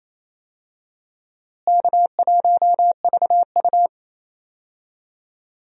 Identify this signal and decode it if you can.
Morse code, heard as a keyed tone, about 28 wpm, 700 Hz. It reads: K1VU
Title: USA Call Signs - 4 Characters - Repeated 3x - 28wpm